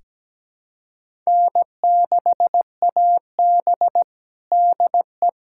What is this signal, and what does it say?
Morse code, heard as a keyed tone, about 17 wpm, 700 Hz. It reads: N6AB DE
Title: USA Callsign DE Callsign - 4 Characters - 17wpm